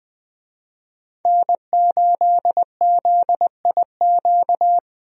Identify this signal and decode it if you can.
Morse code, heard as a keyed tone, about 21 wpm, 700 Hz. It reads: N8ZIQ
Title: USA Call Signs - 5 Characters - Repeated 2x - 20wpm